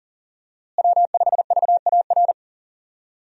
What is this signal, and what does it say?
Morse code, heard as a keyed tone, about 40 wpm, 700 Hz. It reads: W5VAR
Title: USA Call Signs - 5 Characters - Repeated 3x - 40wpm